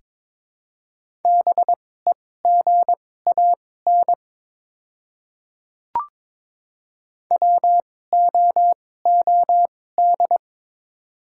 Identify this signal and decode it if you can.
Morse code, heard as a keyed tone, about 21 wpm, spaced as for 16 wpm, 700 Hz. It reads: BEGAN WOOD
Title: 2x Character Spacing - Top 300 Words - 22wpm